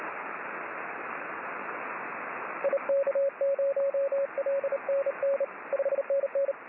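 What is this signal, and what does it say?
Morse code, heard as a keyed tone, about 29 wpm, 550 Hz, with radio noise. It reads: IK0LNN 5NN
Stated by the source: Morse signal on the 6 m band.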